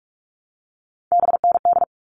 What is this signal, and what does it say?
Morse code, heard as a keyed tone, about 44 wpm, 700 Hz. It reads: BND
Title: All Cut Numbers - 3-Digit 45wpm